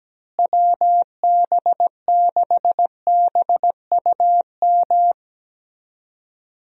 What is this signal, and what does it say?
Morse code, heard as a keyed tone, about 17 wpm, 700 Hz. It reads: WB6BUM
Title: USA Call Signs - 6 Characters - Repeated 2x - 17wpm